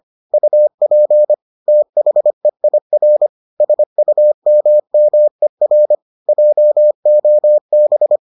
Morse code, heard as a keyed tone, about 25 wpm, 600 Hz. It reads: UP THEIR SUMMER JOB